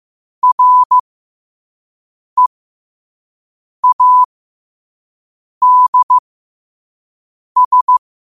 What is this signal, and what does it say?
Morse code, heard as a keyed tone, about 15 wpm, spaced as for 5 wpm, 1000 Hz. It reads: READS
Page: New quotes every day in morse code at 5 Words per minute.